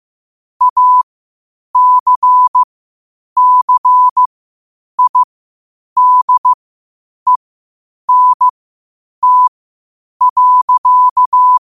Morse code, heard as a keyed tone, about 15 wpm, spaced as for 8 wpm, 1000 Hz. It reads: ACCIDENT.
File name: qotdfiles / 8wpm